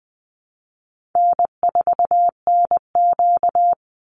Morse code, heard as a keyed tone, about 20 wpm, 700 Hz. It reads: N4NQ
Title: Easy Call Signs Repeated 3x - 20wpm